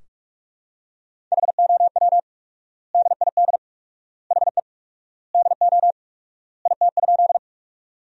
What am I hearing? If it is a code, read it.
Morse code, heard as a keyed tone, about 42 wpm, 700 Hz. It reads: HOW DID HE DO IT?
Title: 4x Word-Spacing - Sentences from Top 100 Words - 45wpm